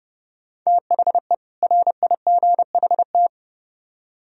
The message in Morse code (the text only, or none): THE RIGHT